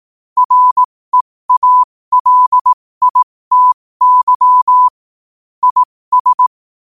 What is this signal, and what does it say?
Morse code, heard as a keyed tone, about 18 wpm, spaced as for 15 wpm, 1000 Hz. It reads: REALITY IS
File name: qotdfiles / 15wpm